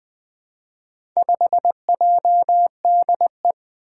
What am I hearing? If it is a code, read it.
Morse code, heard as a keyed tone, about 20 wpm, 700 Hz. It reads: 5JDE
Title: Four Challenging Letter-Number Combinations 20wpm